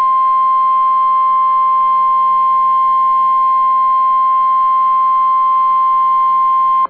Emergency Alert System (SAME) tones (the 1050 Hz attention tone). It is heard from 1.0 to 6.0 s.